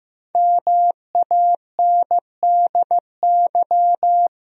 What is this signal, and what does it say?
Morse code, heard as a keyed tone, about 15 wpm, 700 Hz. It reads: MANDY
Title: USA Common Names - Max 5 Letters - 15wpm